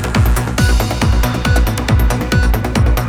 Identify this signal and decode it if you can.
There is no code.